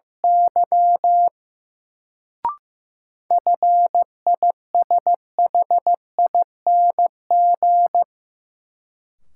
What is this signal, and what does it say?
Morse code, heard as a keyed tone, about 15 wpm, 700 Hz. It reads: Y FISHING
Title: Sentences from Top 1000 Words - Review Part 5 - New Word Drill - 15wpm